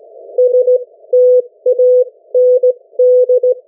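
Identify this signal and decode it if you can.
Morse code, heard as a keyed tone, about 17 wpm, spaced as for 14 wpm, 500 Hz. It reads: STAND
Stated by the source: Morse code from the national amateur radio CW net of New Zealand, which meets on 3535.0 kHz or 7030 kHz.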